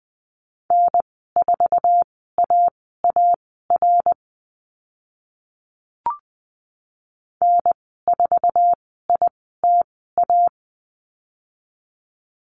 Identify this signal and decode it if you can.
Morse code, heard as a keyed tone, about 20 wpm, spaced as for 14 wpm, 700 Hz. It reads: N4AAR N4ITA
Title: 076 Lesson 13g - Letter L - Call Signs 20wpm